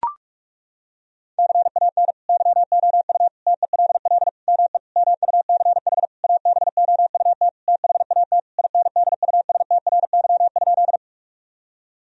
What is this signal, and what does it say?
Morse code, heard as a keyed tone, about 41 wpm, 700 Hz. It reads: CAN YOU TELL ME MUCH ABOUT THAT INDUSTRY?
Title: Sentences from Top 1000 Words - Review Part 4 - 45wpm